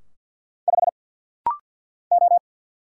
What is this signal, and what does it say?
Morse code, heard as a keyed tone, about 50 wpm, 700 Hz. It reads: 5 O
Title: Single Letter-Number Alt - Rapid-Fire 50wpm